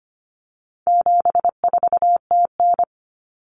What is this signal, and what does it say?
Morse code, heard as a keyed tone, about 25 wpm, 700 Hz. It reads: 74TN
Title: Common Cut Numbers - 4-Digit 25wpm